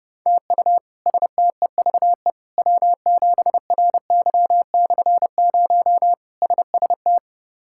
Morse code, heard as a keyed tone, about 30 wpm, 700 Hz. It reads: TU STEVE W7RY/0 SST
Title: K1USN SST QSO Practice - 30wpm